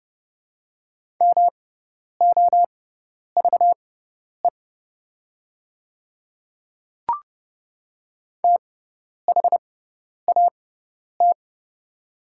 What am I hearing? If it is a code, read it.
Morse code, heard as a keyed tone, about 30 wpm, spaced as for 10 wpm, 700 Hz. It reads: MOVE THAT
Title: 6x Character Spacing - Top 200 Words - 30wpm